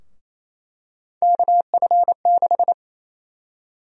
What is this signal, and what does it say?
Morse code, heard as a keyed tone, about 28 wpm, 700 Hz. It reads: KF6
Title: USA Call Sign Prefixes - Repeated 2x - 28wpm